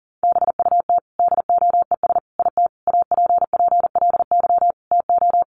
Morse code, heard as a keyed tone, about 40 wpm, 700 Hz. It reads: BUT DOES IT APPLY TO
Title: Sentences from Top 1200 Words 40wpm